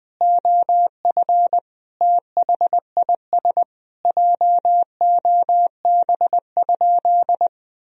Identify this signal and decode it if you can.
Morse code, heard as a keyed tone, about 20 wpm, 700 Hz. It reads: OF THIS JOB?
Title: Sentences from Top 1000 Words - Review Part 3 - 20wpm